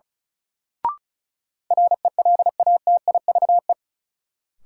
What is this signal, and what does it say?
Morse code, heard as a keyed tone, about 36 wpm, 700 Hz. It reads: RELATIVE
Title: Sentences from Top 1400 Words - New Word Drill - 35wpm